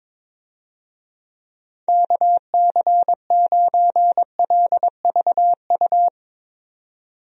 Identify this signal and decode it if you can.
Morse code, heard as a keyed tone, about 22 wpm, 700 Hz. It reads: KC9LVU
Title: USA Call Signs - 6 Characters - 22wpm